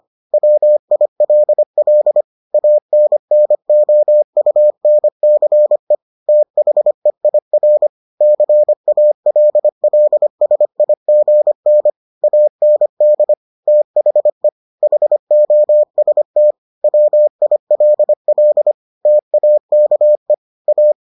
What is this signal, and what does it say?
Morse code, heard as a keyed tone, about 25 wpm, 600 Hz. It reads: WILL ANNOUNCE THEIR CALLSIGN AND THE HOST WILL TAKE A